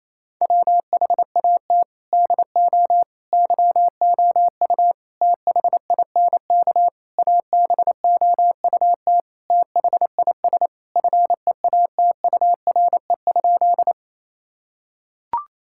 Morse code, heard as a keyed tone, about 28 wpm, 700 Hz. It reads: WHAT DO YOU THINK ABOUT THIS FEATURE?
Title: Sentences from Top 1000 Words - Review Part 5 - 28wpm